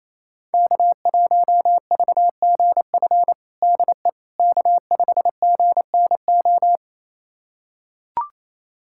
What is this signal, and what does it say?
Morse code, heard as a keyed tone, about 28 wpm, 700 Hz. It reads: K1VGF DE K5GNO
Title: USA Callsign DE Callsign - 5 Characters - 28wpm